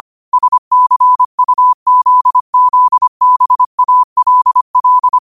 Morse code, heard as a keyed tone, about 25 wpm, 1000 Hz. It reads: SCUZZBALL
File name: qotdfiles / 25wpm